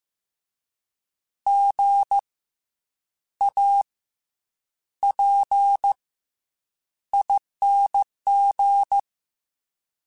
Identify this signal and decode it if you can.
Morse code, heard as a keyed tone, about 15 wpm, 800 Hz. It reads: G A P ING
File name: Special INGs / Special INGs-mp3s / SlowToFaster ing @13